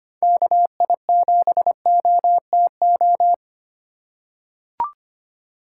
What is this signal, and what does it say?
Morse code, heard as a keyed tone, about 25 wpm, 700 Hz. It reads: KI7OTO
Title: USA Call Signs - 6 Characters - 4x Character Spacing - 25wpm